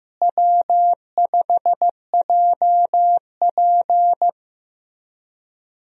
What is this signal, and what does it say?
Morse code, heard as a keyed tone, about 15 wpm, 700 Hz. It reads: W5JP
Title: USA Call Signs - 4 Characters - Repeated 3x - 15wpm